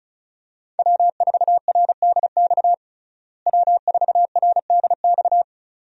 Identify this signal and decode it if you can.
Morse code, heard as a keyed tone, about 35 wpm, 700 Hz. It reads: W4RDX W4RDX
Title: USA Call Signs - 5 Characters - Repeated 2x - 35wpm